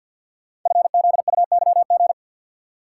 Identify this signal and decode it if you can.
Morse code, heard as a keyed tone, about 52 wpm, 700 Hz. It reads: FZUYG
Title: Code Groups - 5 Letter:Numbers - 50wpm